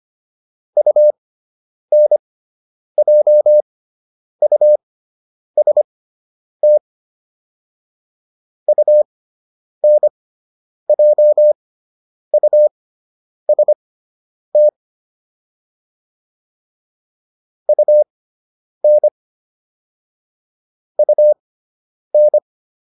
Morse code, heard as a keyed tone, about 25 wpm, spaced as for 9 wpm, 600 Hz. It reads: UNJUST UNJUST UN UN